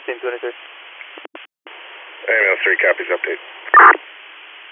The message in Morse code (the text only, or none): none